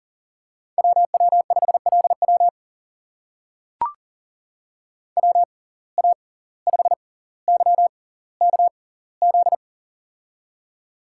Morse code, heard as a keyed tone, about 38 wpm, spaced as for 19 wpm, 700 Hz. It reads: WW5LW WA5YKZ
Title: USA Call Signs - 6x Character Spacing - 40wpm